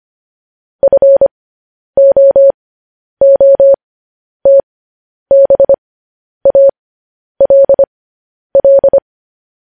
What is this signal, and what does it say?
Morse code, heard as a keyed tone, about 25 wpm, spaced as for 10 wpm, 550 Hz. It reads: FOOTBALL
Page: news-headlines-in-morse-code-at-10-wpm